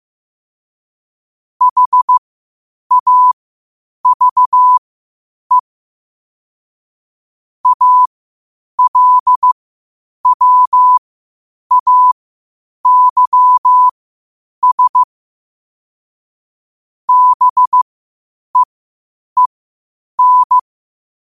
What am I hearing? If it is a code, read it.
Morse code, heard as a keyed tone, about 15 wpm, spaced as for 8 wpm, 1000 Hz. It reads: HAVE ALWAYS BEEN
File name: qotdfiles / 8wpm